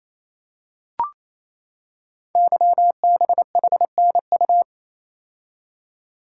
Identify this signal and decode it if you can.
Morse code, heard as a keyed tone, about 29 wpm, 700 Hz. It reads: YBHNU
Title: Code Groups - 5 Letter:Numbers - 28wpm